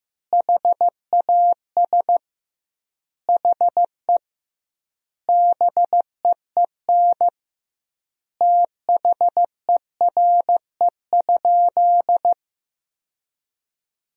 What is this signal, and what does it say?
Morse code, heard as a keyed tone, about 15 wpm, 700 Hz. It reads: HAS HE BEEN THERE?
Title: 2x Word-Spacing - Sentences from Top 100 Words - 15wpm